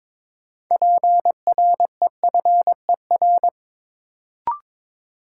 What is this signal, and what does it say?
Morse code, heard as a keyed tone, about 22 wpm, 700 Hz. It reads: PREFER
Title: Sentences from Top 1100 Words - New Word Drill - 22wpm